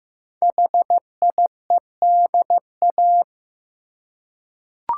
Morse code, heard as a keyed tone, about 15 wpm, 700 Hz. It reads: HIEDA
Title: Code Groups - 5 Letters - 15wpm